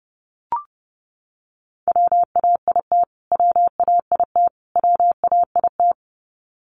Morse code, heard as a keyed tone, about 28 wpm, 700 Hz. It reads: WAIT WAIT WAIT
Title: Top 500 Words Repeated 30wpm